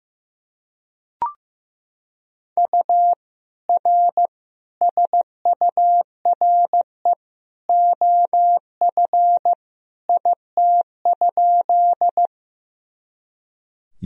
Morse code, heard as a keyed tone, about 15 wpm, 700 Hz. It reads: U R SURE OF IT?
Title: Sentences from Top 300 Words and Abbreviations 15wpm